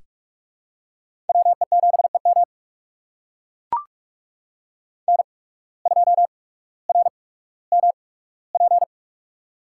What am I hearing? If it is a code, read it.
Morse code, heard as a keyed tone, about 44 wpm, spaced as for 19 wpm, 700 Hz. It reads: WE7EM N2RMP
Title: USA Call Signs - 8x Character Spacing - 45wpm